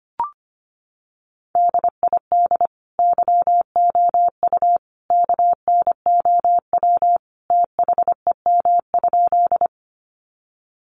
Morse code, heard as a keyed tone, about 25 wpm, 700 Hz. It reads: DID YOU KNOW THEM?
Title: Sentences from Top 100 Words 25wpm